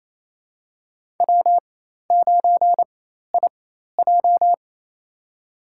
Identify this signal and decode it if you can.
Morse code, heard as a keyed tone, about 28 wpm, spaced as for 13 wpm, 700 Hz. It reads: W9IJ
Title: USA Call Signs - 4 Characters - 4x Character Spacing - 28wpm